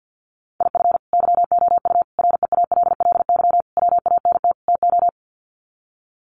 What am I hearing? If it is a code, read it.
Morse code, heard as a keyed tone, about 50 wpm, 700 Hz. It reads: IF YOU REALLY WANT TO
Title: Sets of 5 Words Encore2 50wpm